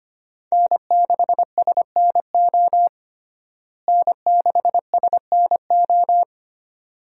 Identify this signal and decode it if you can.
Morse code, heard as a keyed tone, about 25 wpm, 700 Hz. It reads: N6SNO N6SNO
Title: USA Call Signs - 5 Characters - Repeated 2x - 25wpm